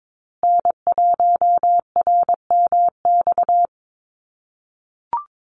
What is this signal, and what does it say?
Morse code, heard as a keyed tone, about 22 wpm, 700 Hz. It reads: N1RMX
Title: USA Call Signs Repeated 2x 22wpm